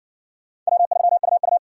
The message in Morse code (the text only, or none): W3UU